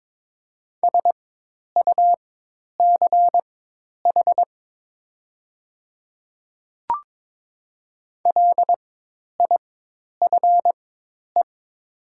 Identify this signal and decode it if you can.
Morse code, heard as a keyed tone, about 22 wpm, spaced as for 10 wpm, 700 Hz. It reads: SUCH LIFE
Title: 4x Character Spacing - Top 300 Words - 22wpm